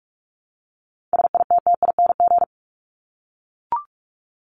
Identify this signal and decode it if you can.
Morse code, heard as a keyed tone, about 44 wpm, 700 Hz. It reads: SITTING
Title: ING Suffix - Top 100 Words with Base Words 45wpm